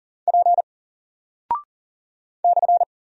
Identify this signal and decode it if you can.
Morse code, heard as a keyed tone, about 40 wpm, 700 Hz. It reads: P /
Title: Single Character (Letter:Number:Punctuation:Prosign) 40wpm